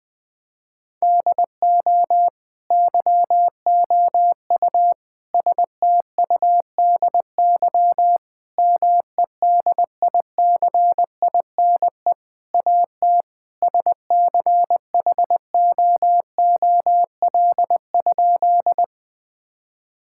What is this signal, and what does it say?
Morse code, heard as a keyed tone, about 21 wpm, 700 Hz. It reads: DO YOU STUDY MEDICINE AT SCHOOL?
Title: Sentences from Top 1000 Words - Review Part 6 - 20wpm